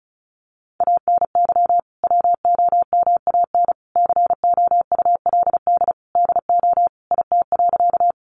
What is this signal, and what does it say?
Morse code, heard as a keyed tone, about 34 wpm, 700 Hz. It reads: ANY WOMAN COULD DO IT.
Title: Sentences from Top 1100 Words 35wpm